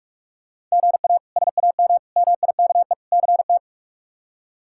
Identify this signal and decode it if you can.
Morse code, heard as a keyed tone, about 43 wpm, 700 Hz. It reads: GA SAM MIKE CT
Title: K1USN SST QSO Practice - 45wpm